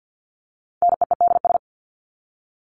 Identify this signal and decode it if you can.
Morse code, heard as a keyed tone, about 50 wpm, 700 Hz. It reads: NEEDS
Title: ES:S Suffix - Top 100 Words with Base Words 50wpm